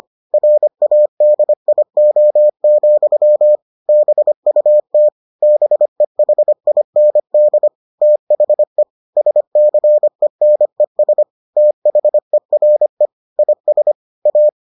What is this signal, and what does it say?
Morse code, heard as a keyed tone, about 25 wpm, 600 Hz. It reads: RADIO, BUT BEHIND THE SCENES THERE IS A